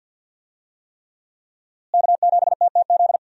Morse code, heard as a keyed tone, about 46 wpm, 700 Hz. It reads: K7TTZ
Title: USA Call Signs - 5 Characters - Repeated 3x - 50wpm